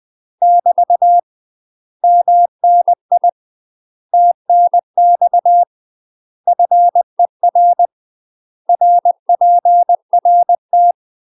Morse code, heard as a keyed tone, about 20 wpm, 700 Hz. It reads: <BT> MNI TNX FER RPRT
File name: UK MASTER QSOs TRAINING / 20WPM / UK MASTER QSOs 0048